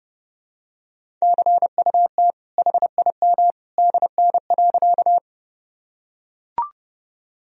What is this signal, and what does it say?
Morse code, heard as a keyed tone, about 30 wpm, 700 Hz. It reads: CUT HIM DN.
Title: Sentences from Top 300 Words and Abbreviations 30wpm